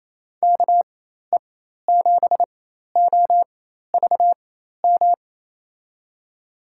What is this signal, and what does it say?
Morse code, heard as a keyed tone, about 28 wpm, spaced as for 13 wpm, 700 Hz. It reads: KE7OVM
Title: USA Call Signs - 6 Characters - 4x Character Spacing - 28wpm